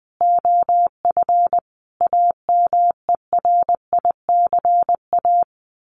Morse code, heard as a keyed tone, about 20 wpm, 700 Hz. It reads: OF AMERICA
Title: Sets of 5 Words 20wpm